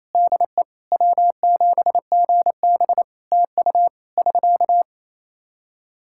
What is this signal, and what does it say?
Morse code, heard as a keyed tone, about 27 wpm, 700 Hz. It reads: DE W7GB TU <SK>
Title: K1USN SST QSO Practice 28wpm